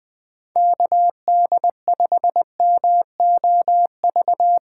Morse code, heard as a keyed tone, about 20 wpm, 700 Hz. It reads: KD5MOV
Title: USA Call Signs - 4x Character Spacing - 20wpm